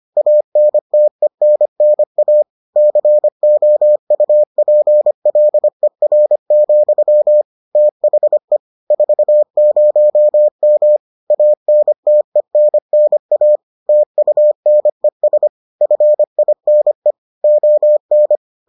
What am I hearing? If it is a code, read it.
Morse code, heard as a keyed tone, about 25 wpm, 600 Hz. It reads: ANTENNA COUPLER, THE 40M ANTENNA TUNES FINE ON